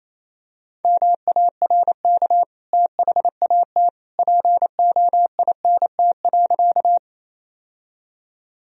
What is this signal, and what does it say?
Morse code, heard as a keyed tone, about 28 wpm, 700 Hz. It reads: MARK THAT POINT.
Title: Sentences from Top 1000 Words - Review Part 1 - 28wpm